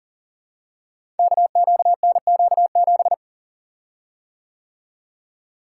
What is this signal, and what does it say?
Morse code, heard as a keyed tone, about 39 wpm, 700 Hz. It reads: KQNQ7
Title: Code Groups - 5 Letter:Numbers - 40wpm